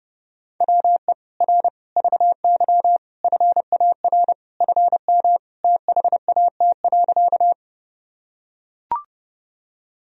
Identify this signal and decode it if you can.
Morse code, heard as a keyed tone, about 32 wpm, 700 Hz. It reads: WE R VY FAR FM THAT.
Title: Sentences from Top 300 Words and Abbreviations 30wpm